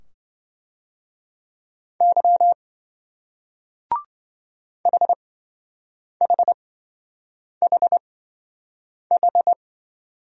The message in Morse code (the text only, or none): Y HHHH